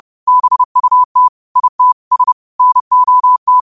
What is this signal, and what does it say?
Morse code, heard as a keyed tone, about 30 wpm, 1000 Hz. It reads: BUT IT S NOT
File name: qotdfiles / 30wpm